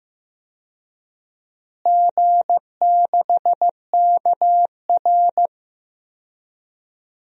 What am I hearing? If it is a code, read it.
Morse code, heard as a keyed tone, about 15 wpm, 700 Hz. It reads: G6KR
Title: DXCC-100 Call Signs - 4 Characters - Repeated 2x - 15wpm